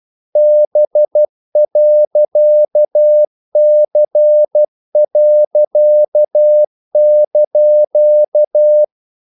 Morse code, heard as a keyed tone, about 12 wpm, 600 Hz. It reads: B.C.)
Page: English quotations in CW (telegraphy - morse)